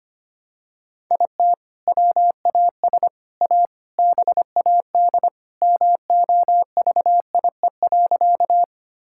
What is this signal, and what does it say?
Morse code, heard as a keyed tone, about 25 wpm, 700 Hz. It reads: IT WAS A BAD MOVIE.
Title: Sentences from Top 1000 Words - Review Part 5 - 25wpm